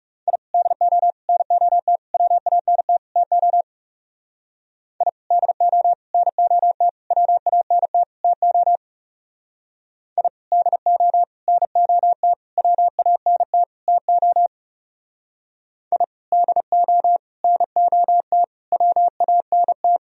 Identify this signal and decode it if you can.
Morse code, heard as a keyed tone, about 37 wpm, 700 Hz. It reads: I DO NOT WANT TO I DO NOT WANT TO I DO NOT WANT TO I DO NOT WANT
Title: Speed-Racing - Sets of 5 Words 30wpm